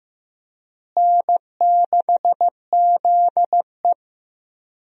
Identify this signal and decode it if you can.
Morse code, heard as a keyed tone, about 15 wpm, 700 Hz. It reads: N6ZE
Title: USA Call Signs - 4 Characters - Repeated 3x - 15wpm